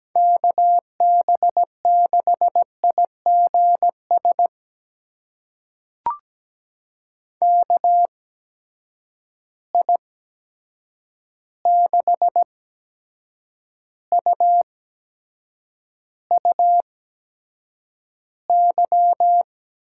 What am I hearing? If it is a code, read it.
Morse code, heard as a keyed tone, about 17 wpm, spaced as for 7 wpm, 700 Hz. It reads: KB6IGS KI6UUY
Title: USA Call Signs - 6 Characters - 8x Character Spacing - 17wpm